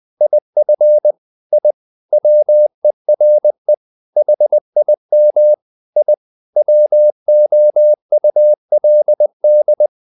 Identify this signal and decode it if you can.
Morse code, heard as a keyed tone, about 20 wpm, 600 Hz. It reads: IF I WERE HIM I WOULD
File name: the old man and the sea / the old man and the sea part 1 20wpm